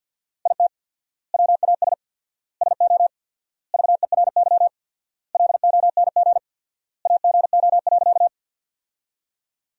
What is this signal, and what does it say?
Morse code, heard as a keyed tone, about 53 wpm, 700 Hz. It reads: IT WAS SO VERY LONG AGO.
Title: 4x Word-Spacing - Sentences from Top 500 Words - 50wpm